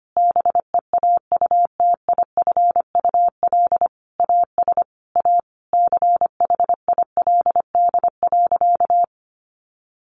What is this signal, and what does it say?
Morse code, heard as a keyed tone, about 25 wpm, 700 Hz. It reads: BEAUTIFUL AS A CHILD.
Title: Sentences from Top 1500 Words 25wpm